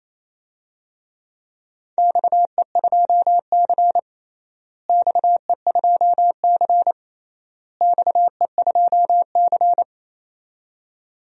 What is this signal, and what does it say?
Morse code, heard as a keyed tone, about 28 wpm, 700 Hz. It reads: XE2C XE2C XE2C
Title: DXCC-100 Call Signs - 4 Characters - Repeated 3x - 28wpm